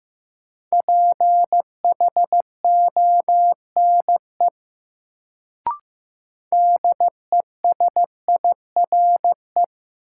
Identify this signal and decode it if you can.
Morse code, heard as a keyed tone, about 15 wpm, 700 Hz. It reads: PHONE DESIRE
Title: Sentences from Top 1100 Words - New Word Drill Alt - 15wpm